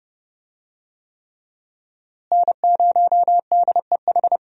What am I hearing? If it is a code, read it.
Morse code, heard as a keyed tone, about 32 wpm, 700 Hz. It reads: N0DEH
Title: USA Call Signs - 5 Characters - Repeated 3x - 30wpm